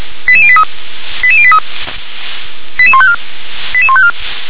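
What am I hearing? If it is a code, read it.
Five-tone selective call (ZVEI), heard as quick runs of five tones, tones of about 70 ms. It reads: 88083 88083 88154 88154
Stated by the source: ZVEI five-tone selective call 88853, SDR recording.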